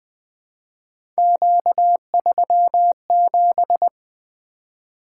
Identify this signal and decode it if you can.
Morse code, heard as a keyed tone, about 20 wpm, 700 Hz. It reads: Q37
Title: Three Letter-Number Combinations - 20wpm